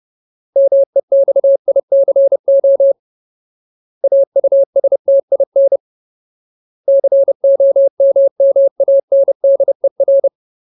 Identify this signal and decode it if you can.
Morse code, heard as a keyed tone, about 30 wpm, 550 Hz. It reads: MEXICO AUSTIN COMMANDER
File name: Ultimate Word List / 30WPM Ultimate